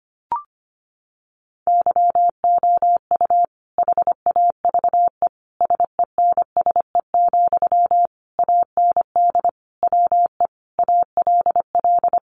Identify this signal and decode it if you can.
Morse code, heard as a keyed tone, about 25 wpm, 700 Hz. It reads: YOU HAVE SENSE, AND WE ALL
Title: Sentences from Top 1100 Words 25wpm